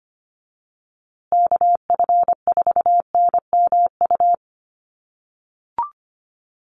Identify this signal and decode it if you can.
Morse code, heard as a keyed tone, about 25 wpm, 700 Hz. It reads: KF4NMU
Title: USA Call Signs Repeated 3x 25wpm